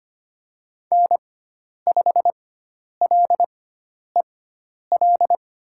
Morse code, heard as a keyed tone, about 25 wpm, spaced as for 10 wpm, 700 Hz. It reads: N5LEL
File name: Lesson 16g - Letter c - Call Signs 25wpm